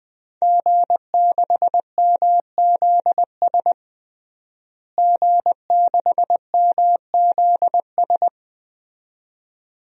Morse code, heard as a keyed tone, about 20 wpm, 700 Hz. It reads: G6MZS G6MZS
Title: DXCC-100 Call Signs - 5 Characters - Repeated 2x - 20wpm